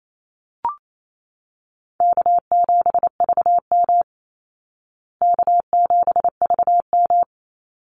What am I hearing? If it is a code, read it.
Morse code, heard as a keyed tone, about 28 wpm, 700 Hz. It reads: K7VM K7VM
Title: Easy Call Signs Repeated 2x - 28wpm